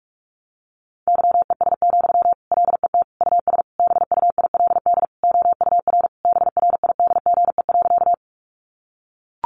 Morse code, heard as a keyed tone, about 44 wpm, 700 Hz. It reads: YES, LET US BUILD OUR BRIDGE.
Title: Sentences from Top 1200 Words 45wpm